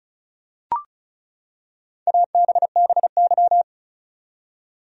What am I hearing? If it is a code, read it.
Morse code, heard as a keyed tone, about 34 wpm, 700 Hz. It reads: ABBY
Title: USA Common Names - Max 5 Letters - 35wpm